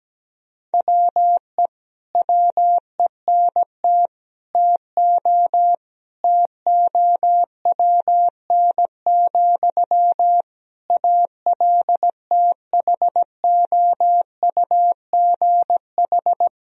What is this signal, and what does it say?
Morse code, heard as a keyed tone, about 17 wpm, 700 Hz. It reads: WE WENT TO TOWN, ALTHOUGH
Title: Sentences from Top 1000 Words - Review Part 3 - 17wpm